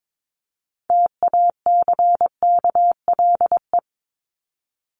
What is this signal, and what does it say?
Morse code, heard as a keyed tone, about 22 wpm, 700 Hz. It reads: TACKLE
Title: Top 1751-2000 Words 22wpm